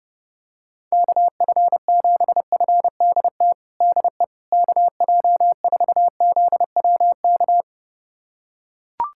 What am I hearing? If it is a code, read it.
Morse code, heard as a keyed tone, about 30 wpm, 700 Hz. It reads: KF7FDT DE KJ4ZWK
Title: USA Callsign DE Callsign - 6 Characters - 30wpm